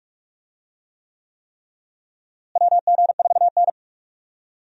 Morse code, heard as a keyed tone, about 42 wpm, 700 Hz. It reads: WG4N
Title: USA Call Signs - 4 Characters - Repeated 3x - 45wpm